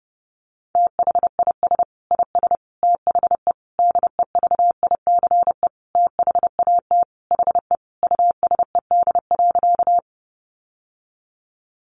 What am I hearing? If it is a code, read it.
Morse code, heard as a keyed tone, about 28 wpm, 700 Hz. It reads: THIS IS THE DEVICE THAT HE USED.
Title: Sentences from Top 1500 Words 30wpm